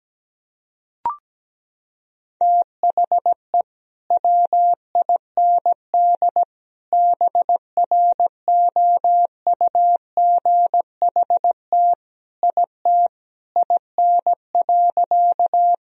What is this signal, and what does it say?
Morse code, heard as a keyed tone, about 17 wpm, 700 Hz. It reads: THE WIND BROUGHT IT IN.
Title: Sentences from Top 1000 Words - Review Part 2 - 17wpm